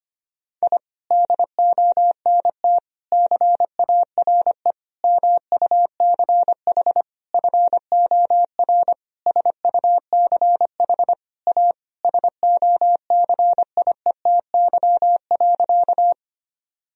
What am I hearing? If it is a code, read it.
Morse code, heard as a keyed tone, about 25 wpm, 700 Hz. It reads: I DONT CARE MUCH FOR SUCH A SOCIETY.
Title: Sentences from Top 1000 Words - Review Part 4 - 25wpm